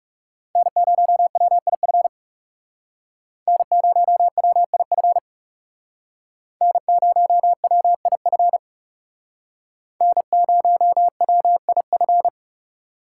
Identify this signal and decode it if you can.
Morse code, heard as a keyed tone, about 38 wpm, 700 Hz. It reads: N0WIF N0WIF N0WIF N0WIF